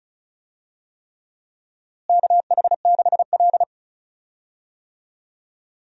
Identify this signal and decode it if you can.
Morse code, heard as a keyed tone, about 34 wpm, 700 Hz. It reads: KH6L
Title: DXCC-100 Call Signs - 4 Characters - Repeated 3x - 35wpm